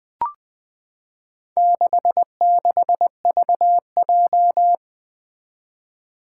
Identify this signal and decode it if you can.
Morse code, heard as a keyed tone, about 20 wpm, 700 Hz. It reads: 66VJ
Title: Four Challenging Letter-Number Combinations 20wpm